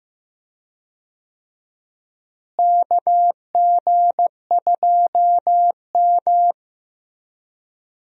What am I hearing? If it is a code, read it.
Morse code, heard as a keyed tone, about 16 wpm, 700 Hz. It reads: KG2M
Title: USA Call Signs - 4 Characters - Repeated 2x - 15wpm